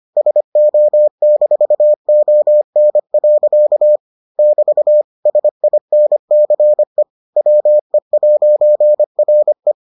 Morse code, heard as a keyed tone, about 25 wpm, 600 Hz. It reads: SO-ON. <BT> SINCE WE'RE